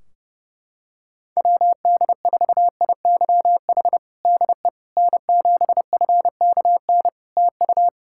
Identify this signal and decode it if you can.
Morse code, heard as a keyed tone, about 30 wpm, 700 Hz. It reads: WD4IYH DE N7FKN TU
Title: Practice QSO - Exchange 4 - Ending - 30wpm